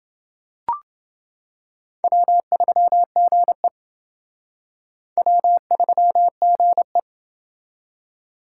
Morse code, heard as a keyed tone, about 28 wpm, 700 Hz. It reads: W3GE W3GE